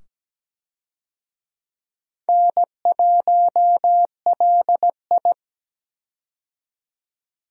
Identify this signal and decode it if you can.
Morse code, heard as a keyed tone, about 17 wpm, 700 Hz. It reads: N1LI
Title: USA Call Signs - 4 Characters - 17wpm